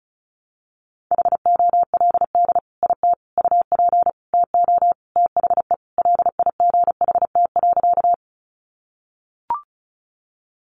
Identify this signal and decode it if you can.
Morse code, heard as a keyed tone, about 34 wpm, 700 Hz. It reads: HOLD IT UP TO THE LIGHT.
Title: Sentences from Top 400 Words 35wpm